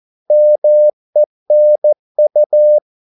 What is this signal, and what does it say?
Morse code, heard as a keyed tone, about 14 wpm, 600 Hz. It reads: MENU
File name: Korte ord DK 14wpm